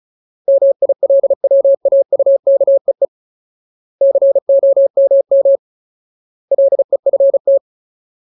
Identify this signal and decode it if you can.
Morse code, heard as a keyed tone, about 34 wpm, 550 Hz. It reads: MILWAUKEE COMM LEFT
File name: Ultimate Word List / 35WPM Ultimate